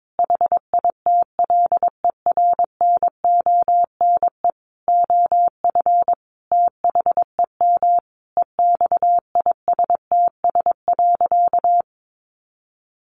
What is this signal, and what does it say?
Morse code, heard as a keyed tone, about 22 wpm, 700 Hz. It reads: HITLER NONE OF THEM EXISTS.
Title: A Sound Of Thunder 22wpm